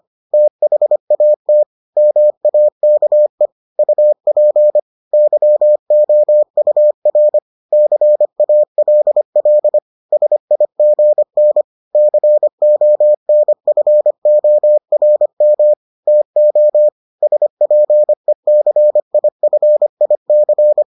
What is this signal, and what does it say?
Morse code, heard as a keyed tone, about 25 wpm, 600 Hz. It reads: THAT MAKE UP YOUR CALL SIGN CONFORM TO SPECIFIC